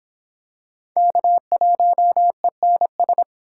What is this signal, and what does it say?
Morse code, heard as a keyed tone, about 26 wpm, 700 Hz. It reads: K1ENS